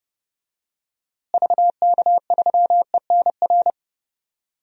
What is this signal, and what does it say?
Morse code, heard as a keyed tone, about 29 wpm, 700 Hz. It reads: VK3ENR
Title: DXCC-100 Call Signs - 6 Characters - Repeated 3x - 30wpm